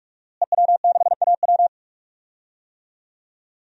Morse code, heard as a keyed tone, about 42 wpm, 700 Hz. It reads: EW6AW
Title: DXCC-100 Call Signs - 5 Characters - Repeated 2x - 45wpm